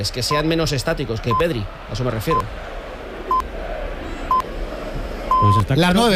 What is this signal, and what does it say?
Radio time signal (time pips), with speech. It is heard from 0.3 to 5.6 s.